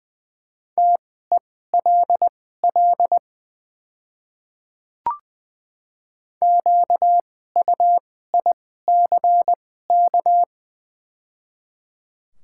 Morse code, heard as a keyed tone, about 20 wpm, spaced as for 14 wpm, 700 Hz. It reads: TELL QUICK
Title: 2x Character Spacing - Top 500 Words - 20wpm